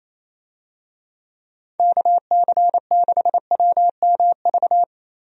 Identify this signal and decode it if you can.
Morse code, heard as a keyed tone, about 28 wpm, 700 Hz. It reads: KC6WMV
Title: USA Call Signs - 6 Characters - Repeated 3x - 28wpm